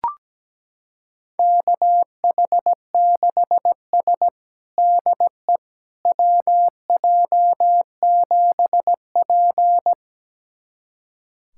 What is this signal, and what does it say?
Morse code, heard as a keyed tone, about 17 wpm, 700 Hz. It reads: KH6S DE WJ7P
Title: USA Callsign DE Callsign - 4 Characters - 17wpm